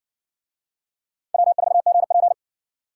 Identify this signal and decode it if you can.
Morse code, heard as a keyed tone, about 56 wpm, 700 Hz. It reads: W4GP